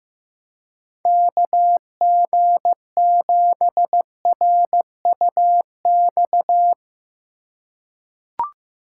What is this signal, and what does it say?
Morse code, heard as a keyed tone, about 15 wpm, 700 Hz. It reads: KG7RUX
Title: USA Call Signs - 6 Characters - 3x Character Spacing - 15wpm